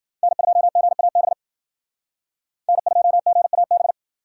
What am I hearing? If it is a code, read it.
Morse code, heard as a keyed tone, about 50 wpm, 700 Hz. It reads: N2GAB N2GAB